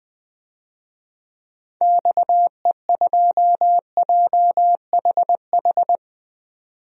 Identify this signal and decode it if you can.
Morse code, heard as a keyed tone, about 21 wpm, 700 Hz. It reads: XE2JHH
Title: DXCC-100 Call Signs - 6 Characters - 20wpm